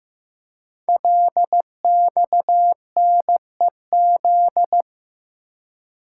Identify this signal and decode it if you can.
Morse code, heard as a keyed tone, about 15 wpm, 700 Hz. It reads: LXNEZ
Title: Code Groups - 5 Letters - 15wpm